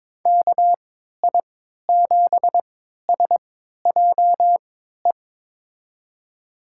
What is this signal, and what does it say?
Morse code, heard as a keyed tone, about 22 wpm, spaced as for 13 wpm, 700 Hz. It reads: KI7SJE
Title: USA Call Signs - 3x Character Spacing - 22wpm